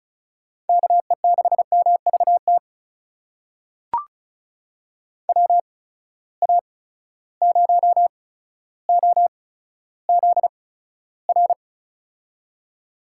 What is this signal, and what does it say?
Morse code, heard as a keyed tone, about 34 wpm, spaced as for 15 wpm, 700 Hz. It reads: KE6MVT WA0OZR
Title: USA Call Signs - 6 Characters - 8x Character Spacing - 35wpm